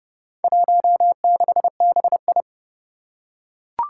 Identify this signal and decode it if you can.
Morse code, heard as a keyed tone, about 30 wpm, 700 Hz. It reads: 16BI
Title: Four Challenging Letter-Number Combinations 30wpm